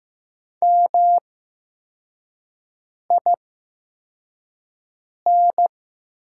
Morse code, heard as a keyed tone, about 15 wpm, spaced as for 4 wpm, 700 Hz. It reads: MIN
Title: 8x Character Spacing - Top 500 Words - 15wpm